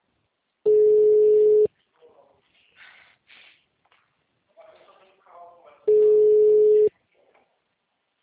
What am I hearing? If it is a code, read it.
Ringback tone (Europe, 425 Hz), with speech.
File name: out- -2014- - - .2